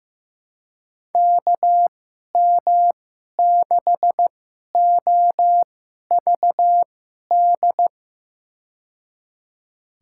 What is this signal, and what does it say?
Morse code, heard as a keyed tone, about 15 wpm, spaced as for 11 wpm, 700 Hz. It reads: KM6OVD
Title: USA Call Signs - 6 Characters - 2x Character Spacing - 15wpm